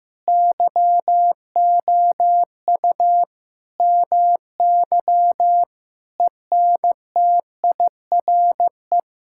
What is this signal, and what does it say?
Morse code, heard as a keyed tone, about 15 wpm, 700 Hz. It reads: YOU MY ENTIRE
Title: Sentences from Top 1000 Words - Review Part 5 - 15wpm